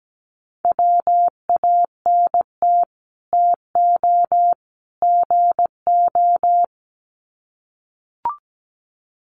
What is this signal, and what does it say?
Morse code, heard as a keyed tone, about 17 wpm, 700 Hz. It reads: WANT TO GO
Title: Sets of 3 Words 17wpm